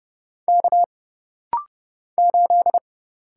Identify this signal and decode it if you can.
Morse code, heard as a keyed tone, about 30 wpm, 700 Hz. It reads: K 8
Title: Single Letter-Number - Rapid-Fire 30wpm